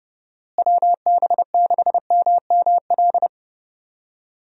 Morse code, heard as a keyed tone, about 32 wpm, 700 Hz. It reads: WB6MML
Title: USA Call Signs - 6 Characters - Repeated 3x - 30wpm